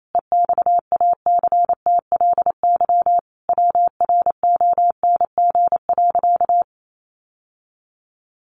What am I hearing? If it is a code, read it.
Morse code, heard as a keyed tone, about 28 wpm, 700 Hz. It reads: EXACTLY WRONG.
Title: Sentences from Top 900 Words 28wpm